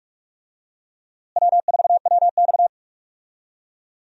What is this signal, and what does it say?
Morse code, heard as a keyed tone, about 41 wpm, 700 Hz. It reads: W4WX
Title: USA Call Signs - 4 Characters - Repeated 3x - 45wpm